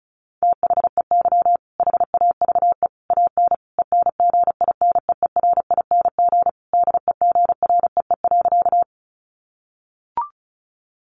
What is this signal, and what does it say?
Morse code, heard as a keyed tone, about 34 wpm, 700 Hz. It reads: THEY HAVE AN ENGINEERING DEGREE.
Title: Sentences from Top 1500 Words 35wpm